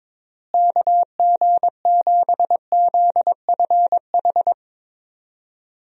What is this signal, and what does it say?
Morse code, heard as a keyed tone, about 22 wpm, 700 Hz. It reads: KG7ZFH
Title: USA Call Signs - 6 Characters - Repeated 2x - 22wpm